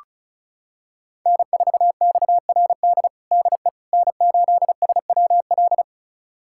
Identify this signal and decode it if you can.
Morse code, heard as a keyed tone, about 34 wpm, 700 Hz. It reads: N4XRD DE N8SWL
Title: USA Callsign DE Callsign 35wpm